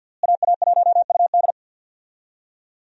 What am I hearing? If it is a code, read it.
Morse code, heard as a keyed tone, about 50 wpm, 700 Hz. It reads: AA1UD
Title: USA Call Signs - 5 Characters - Repeated 2x - 50wpm